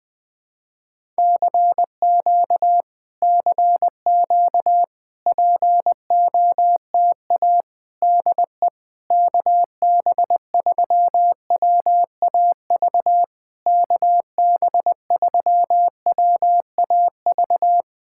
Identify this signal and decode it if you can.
Morse code, heard as a keyed tone, about 20 wpm, 690 Hz. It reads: CQ CQ POTA DE KB3WAV KB3WAV
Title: POTA QSO Practice - 20wpm